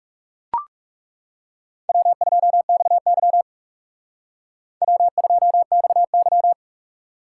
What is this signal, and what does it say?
Morse code, heard as a keyed tone, about 36 wpm, 700 Hz. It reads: W2XY W2XY
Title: Speed Racing - USA 1x2 Call Signs 30wpm